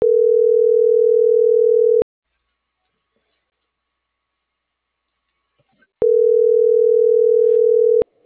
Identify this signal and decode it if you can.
Ringback tone (North America), 440 + 480 Hz.